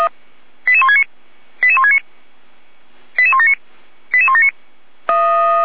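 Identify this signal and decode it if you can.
Five-tone selective call (ZVEI), heard as quick runs of five tones, tones of about 70 ms. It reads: 88169 88169 88179 88179
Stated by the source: ZVEI five-tone selective call received on an SDR.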